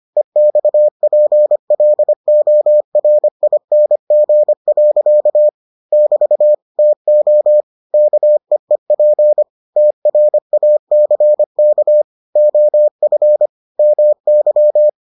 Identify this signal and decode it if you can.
Morse code, heard as a keyed tone, about 25 wpm, 600 Hz. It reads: EXPLORING. <BT> TO KEEP TRACK OF MY